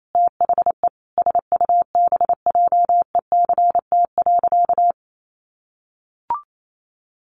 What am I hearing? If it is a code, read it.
Morse code, heard as a keyed tone, about 28 wpm, 700 Hz. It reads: THE SUBJECT.
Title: Sentences from Top 1400 Words 28wpm